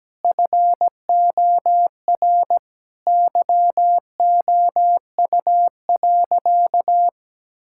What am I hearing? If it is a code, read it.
Morse code, heard as a keyed tone, about 17 wpm, 700 Hz. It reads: FOR YOU.
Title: Sentences from Top 1000 Words - Review Part 6 - 17wpm